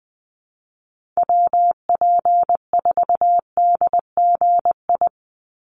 Morse code, heard as a keyed tone, about 20 wpm, 700 Hz. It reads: WP4DGI
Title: USA Call Signs 20wpm